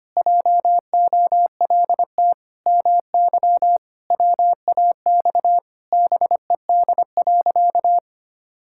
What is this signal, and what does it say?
Morse code, heard as a keyed tone, about 25 wpm, 700 Hz. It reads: JOLT MY WAX BED.
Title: Pangram Challenge v2 - 25wpm